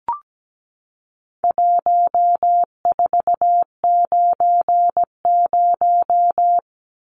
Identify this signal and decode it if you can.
Morse code, heard as a keyed tone, about 17 wpm, 700 Hz. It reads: 1490
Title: Numbers - 4-Digit 17wpm